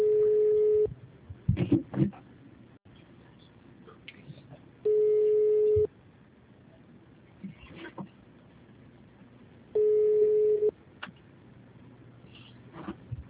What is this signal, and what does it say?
Ringback tone (Europe, 425 Hz).
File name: out- -2031- - - .8226